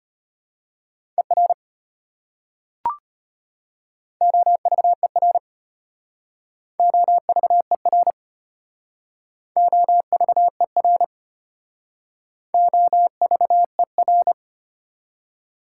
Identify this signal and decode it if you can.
Morse code, heard as a keyed tone, about 32 wpm, 690 Hz. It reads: ER OVER OVER OVER OVER